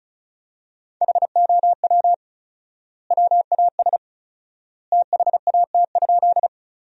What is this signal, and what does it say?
Morse code, heard as a keyed tone, about 36 wpm, 700 Hz. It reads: HOW WAS THAT?
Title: 4x Word-Spacing - Sentences from Top 100 Words - 35wpm